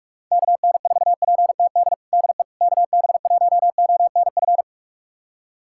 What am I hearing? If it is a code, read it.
Morse code, heard as a keyed tone, about 46 wpm, 700 Hz. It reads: KN4PTD DE KB1ONF
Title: USA Callsign DE Callsign - 6 Characters - 45wpm